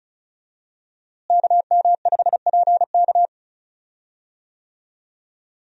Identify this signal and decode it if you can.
Morse code, heard as a keyed tone, about 36 wpm, 700 Hz. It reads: KM5PK
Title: USA Call Signs - 5 Characters - Repeated 3x - 35wpm